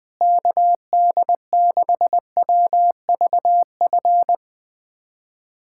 Morse code, heard as a keyed tone, about 20 wpm, 700 Hz. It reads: KD6WVF
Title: USA Callsign DE Callsign - 6 Characters - 20wpm